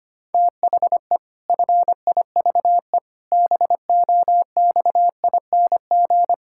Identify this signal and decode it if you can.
Morse code, heard as a keyed tone, about 25 wpm, 700 Hz. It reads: THE FIVE BOXING
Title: Pangram Challenge v2 - 25wpm